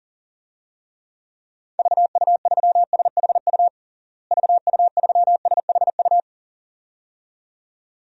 Morse code, heard as a keyed tone, about 40 wpm, 700 Hz. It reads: VU3SHU VU3SHU
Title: DXCC-100 Call Signs - 6 Characters - Repeated 2x - 40wpm